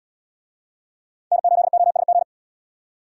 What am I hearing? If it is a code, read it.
Morse code, heard as a keyed tone, about 73 wpm, 700 Hz. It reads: N8OSG